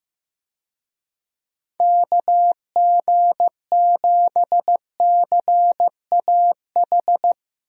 Morse code, heard as a keyed tone, about 15 wpm, 700 Hz. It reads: KG7CAH
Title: USA Call Signs - 6 Characters - Repeated 2x - 15wpm